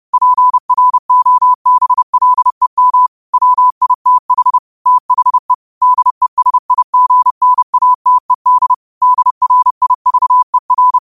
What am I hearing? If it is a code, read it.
Morse code, heard as a keyed tone, about 30 wpm, 1000 Hz. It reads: PROBLEM WITH THE DESIGNATED DRIVER
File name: qotdfiles / 30wpm